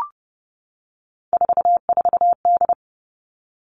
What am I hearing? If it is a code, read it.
Morse code, heard as a keyed tone, about 30 wpm, 700 Hz. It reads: 44D
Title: All Cut Numbers - 3-Digit 30wpm